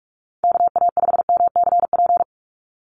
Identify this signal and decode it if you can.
Morse code, heard as a keyed tone, about 44 wpm, 700 Hz. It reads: KA5MCP
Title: USA Call Signs 45wpm